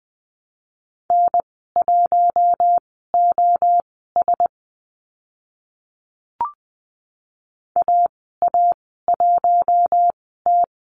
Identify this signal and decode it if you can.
Morse code, heard as a keyed tone, about 20 wpm, spaced as for 14 wpm, 700 Hz. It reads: N1OS AA1T
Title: 042 Lesson 8g - Number 1 - Call Signs 20wpm